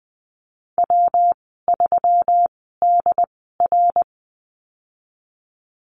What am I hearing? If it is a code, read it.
Morse code, heard as a keyed tone, about 20 wpm, spaced as for 15 wpm, 700 Hz. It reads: W3DR
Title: 140 Lesson 23g - Letter f - Call Signs 20wpm